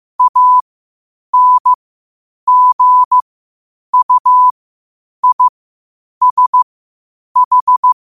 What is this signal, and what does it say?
Morse code, heard as a keyed tone, about 15 wpm, spaced as for 8 wpm, 1000 Hz. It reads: ANGUISH
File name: qotdfiles / 8wpm